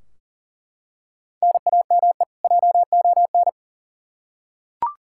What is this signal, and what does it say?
Morse code, heard as a keyed tone, about 40 wpm, 700 Hz. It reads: NAME JON
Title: K1USN SST QSO Practice 40wpm